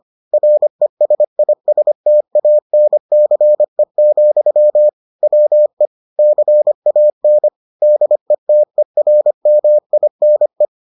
Morse code, heard as a keyed tone, about 25 wpm, 600 Hz. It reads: RESISTANCE, WE CAN DETERMINE